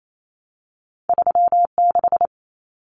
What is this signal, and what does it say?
Morse code, heard as a keyed tone, about 28 wpm, 700 Hz. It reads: 3 6
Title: Common Cut Numbers - 2-Digit 28wpm